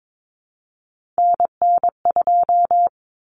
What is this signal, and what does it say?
Morse code, heard as a keyed tone, about 22 wpm, 700 Hz. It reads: NN2
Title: Common Cut Numbers - 3-Digit 22wpm